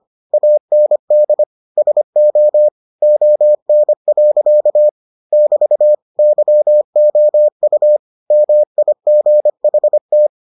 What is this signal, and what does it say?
Morse code, heard as a keyed tone, about 25 wpm, 600 Hz. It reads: AND SO ON. <BT> YOU MIGHT